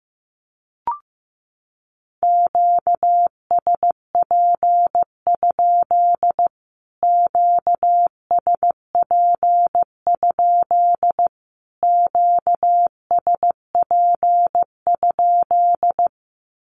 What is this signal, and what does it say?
Morse code, heard as a keyed tone, about 15 wpm, 700 Hz. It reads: QSP? QSP? QSP?
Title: Q-Codes Repeated 3x 15wpm